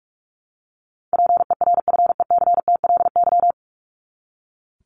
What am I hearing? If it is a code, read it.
Morse code, heard as a keyed tone, about 44 wpm, 700 Hz. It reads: PERFECTLY
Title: Top 1501-1750 Words 45wpm